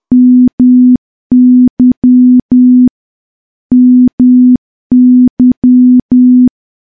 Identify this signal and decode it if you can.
Morse code, heard as a keyed tone, about 10 wpm, 250 Hz. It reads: MY MY
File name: morse / morse-training-10WPM